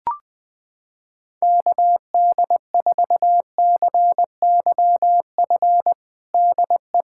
Morse code, heard as a keyed tone, about 21 wpm, 700 Hz. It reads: KD4CYF DE
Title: USA Callsign DE Callsign - 6 Characters - 20wpm